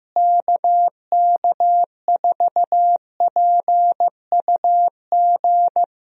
Morse code, heard as a keyed tone, about 15 wpm, 700 Hz. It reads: KK4PUG
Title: USA Call Signs - 6 Characters - Repeated 2x - 15wpm